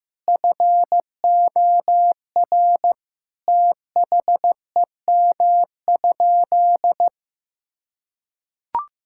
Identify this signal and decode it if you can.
Morse code, heard as a keyed tone, about 15 wpm, 690 Hz. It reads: FOR THEM?
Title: Sentences from Top 1000 Words - Review Part 5 - 15wpm